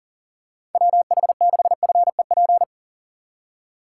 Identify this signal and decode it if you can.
Morse code, heard as a keyed tone, about 40 wpm, 700 Hz. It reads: WH6FEP
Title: USA Call Signs - 6 Characters - 4x Character Spacing - 40wpm